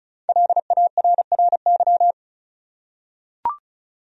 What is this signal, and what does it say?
Morse code, heard as a keyed tone, about 34 wpm, 700 Hz. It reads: LARRY
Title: USA Common Names - 35wpm